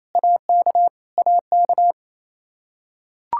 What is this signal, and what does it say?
Morse code, heard as a keyed tone, about 28 wpm, 700 Hz. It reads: AK AK
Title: Practice QSO - Exchange 1 - Beginning - 28wpm